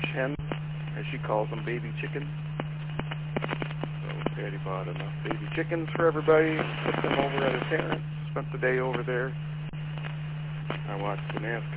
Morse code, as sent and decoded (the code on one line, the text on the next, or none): none